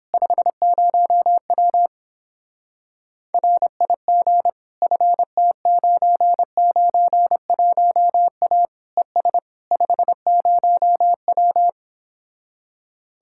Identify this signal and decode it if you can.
Morse code, heard as a keyed tone, about 29 wpm, 700 Hz. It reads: 50W RIG FT991A ES 50W